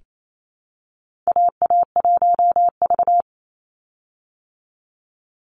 Morse code, heard as a keyed tone, about 28 wpm, 700 Hz. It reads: AA1V
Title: Call Signs Repeated 3x - 28wpm